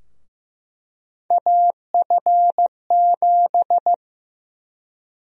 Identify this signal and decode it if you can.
Morse code, heard as a keyed tone, about 15 wpm, 700 Hz. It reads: AF7
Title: USA Call Sign Prefixes - 15wpm